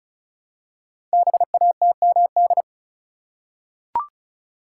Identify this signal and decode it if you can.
Morse code, heard as a keyed tone, about 36 wpm, 700 Hz. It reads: BATMD
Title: Code Groups - 5 Letters - 35wpm